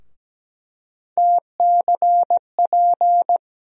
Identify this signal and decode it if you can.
Morse code, heard as a keyed tone, about 17 wpm, 700 Hz. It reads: TCP
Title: USA Call Sign Postfixes - 3 Letters 17wpm